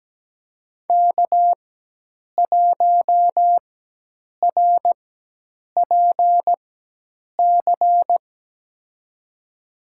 Morse code, heard as a keyed tone, about 17 wpm, spaced as for 8 wpm, 700 Hz. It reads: K1RPC
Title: USA Call Signs - 5 Characters - 4x Character Spacing - 17wpm